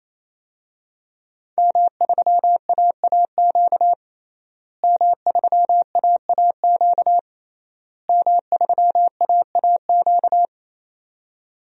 Morse code, heard as a keyed tone, about 28 wpm, 700 Hz. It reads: M3AAQ M3AAQ M3AAQ
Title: DXCC-100 Call Signs - 5 Characters - Repeated 3x - 28wpm